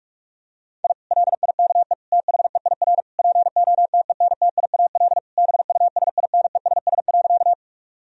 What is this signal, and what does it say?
Morse code, heard as a keyed tone, about 46 wpm, 700 Hz. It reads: I LIKE THEIR POTENTIAL BUSINESS.
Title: Sentences from Top 1000 Words - Review Part 4 - 45wpm